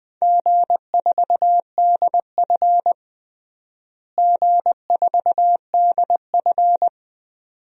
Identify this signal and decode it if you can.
Morse code, heard as a keyed tone, about 20 wpm, 700 Hz. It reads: G4DF G4DF
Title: DXCC-100 Call Signs - 4 Characters - Repeated 2x - 20wpm